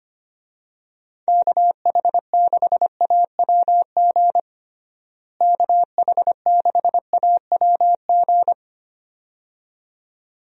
Morse code, heard as a keyed tone, about 25 wpm, 700 Hz. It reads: KH6AWG KH6AWG
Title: DXCC-100 Call Signs - 6 Characters - Repeated 2x - 25wpm